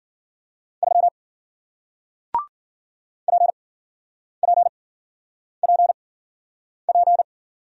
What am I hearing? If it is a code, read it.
Morse code, heard as a keyed tone, about 48 wpm, spaced as for 9 wpm, 700 Hz. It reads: 3 PPPP
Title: Speed-Racing Single Letter-Number 40wpm